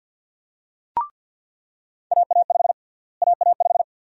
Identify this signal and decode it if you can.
Morse code, heard as a keyed tone, about 47 wpm, 700 Hz. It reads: AA5 AA5
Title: USA Call Sign Prefixes - Repeated 2x - 50wpm